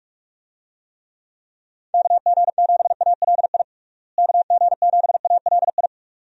Morse code, heard as a keyed tone, about 46 wpm, 700 Hz. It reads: KG7ALI KG7ALI
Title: USA Call Signs - 6 Characters - Repeated 2x - 45wpm